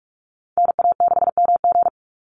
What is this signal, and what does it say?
Morse code, heard as a keyed tone, about 46 wpm, 700 Hz. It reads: NA6MG
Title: NA SOTA Activator Callsigns 45wpm